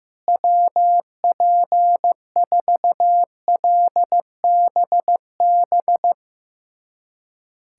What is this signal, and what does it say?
Morse code, heard as a keyed tone, about 15 wpm, 700 Hz. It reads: WP4LBB
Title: USA Callsign DE Callsign - 6 Characters - 15wpm